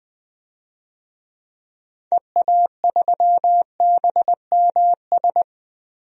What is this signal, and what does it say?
Morse code, heard as a keyed tone, about 20 wpm, 700 Hz. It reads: EA3BMS
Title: DXCC-100 Call Signs - 6 Characters - Repeated 2x - 20wpm